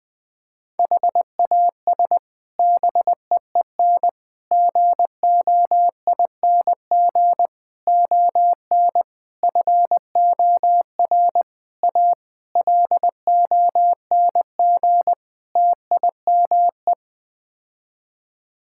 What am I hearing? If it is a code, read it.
Morse code, heard as a keyed tone, about 20 wpm, 700 Hz. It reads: HAS BEEN GOING ON FOR A LONG TIME
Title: Sets of 8 Words - 20wpm